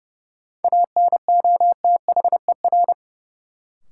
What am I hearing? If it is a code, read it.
Morse code, heard as a keyed tone, about 31 wpm, 700 Hz. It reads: ANOTHER
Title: Sets of 5 Words Encore - New Word Drill - 30wpm